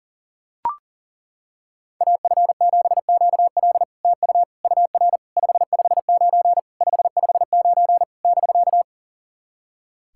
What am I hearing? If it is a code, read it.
Morse code, heard as a keyed tone, about 40 wpm, 690 Hz. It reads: AF7QL TU UR 559 559 <BK>
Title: SOTA QSO Practice 40wpm